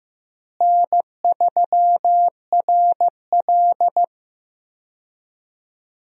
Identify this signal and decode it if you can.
Morse code, heard as a keyed tone, about 15 wpm, 700 Hz. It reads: N3RL
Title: USA Call Signs - 4 Characters - Repeated 3x - 15wpm